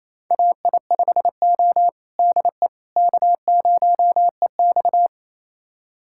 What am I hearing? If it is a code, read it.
Morse code, heard as a keyed tone, about 28 wpm, 700 Hz. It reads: AI5O DE K0EX
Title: USA Callsign DE Callsign - 4 Characters - 28wpm